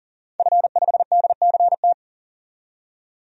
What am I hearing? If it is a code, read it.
Morse code, heard as a keyed tone, about 40 wpm, 700 Hz. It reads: F5DCT
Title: DXCC-100 Call Signs - 5 Characters - 40wpm